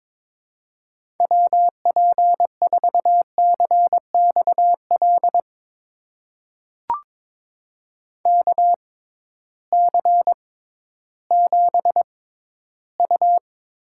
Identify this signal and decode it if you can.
Morse code, heard as a keyed tone, about 23 wpm, spaced as for 13 wpm, 700 Hz. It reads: WP4CXL KC7U
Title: USA Call Signs - 6 Characters - 6x Character Spacing - 22wpm